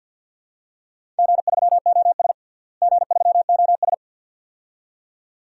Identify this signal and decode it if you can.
Morse code, heard as a keyed tone, about 51 wpm, 700 Hz. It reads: G3OS G3OS
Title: DXCC-100 Call Signs - 4 Characters - Repeated 2x - 50wpm